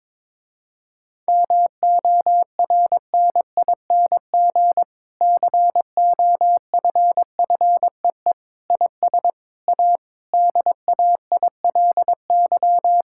Morse code, heard as a keyed tone, about 22 wpm, 700 Hz. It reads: MORNING COFFEE IS A DAILY
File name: Difficult Sentences with Hints - 22wpm